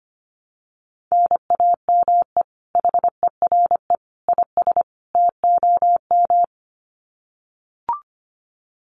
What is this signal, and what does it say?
Morse code, heard as a keyed tone, about 25 wpm, 700 Hz. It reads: NAME HERE IS TOM
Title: Common QSO Phrases 25wpm